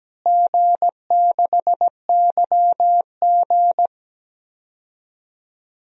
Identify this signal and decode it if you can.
Morse code, heard as a keyed tone, about 17 wpm, 700 Hz. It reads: G6YG
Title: DXCC-100 Call Signs - 4 Characters - Repeated 2x - 17wpm